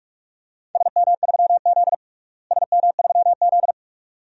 Morse code, heard as a keyed tone, about 46 wpm, 700 Hz. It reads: SM3Z SM3Z
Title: DXCC-100 Call Signs - 4 Characters - Repeated 2x - 45wpm